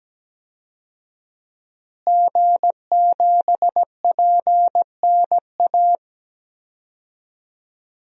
Morse code, heard as a keyed tone, about 17 wpm, 700 Hz. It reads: G7PNA
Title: DXCC-100 Call Signs - 5 Characters - Repeated 3x - 17wpm